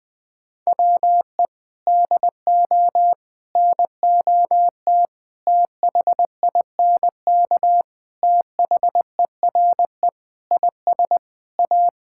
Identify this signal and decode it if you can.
Morse code, heard as a keyed tone, about 20 wpm, 700 Hz. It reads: WE DO NOT THINK THERE IS A
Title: Sentences with 4-Word Phrases - Spoken Beginning - 20wpm